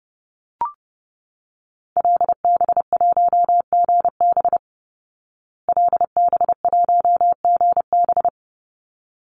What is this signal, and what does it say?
Morse code, heard as a keyed tone, about 30 wpm, 700 Hz. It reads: LB1GB LB1GB
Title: Call Signs Repeated 3x - 30wpm